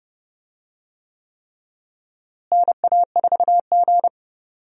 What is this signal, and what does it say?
Morse code, heard as a keyed tone, about 30 wpm, 700 Hz. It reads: NA4G
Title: USA Call Signs - 4 Characters - Repeated 3x - 30wpm